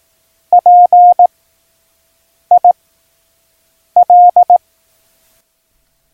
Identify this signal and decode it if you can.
Morse code, heard as a keyed tone, about 18 wpm, 710 Hz. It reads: P I L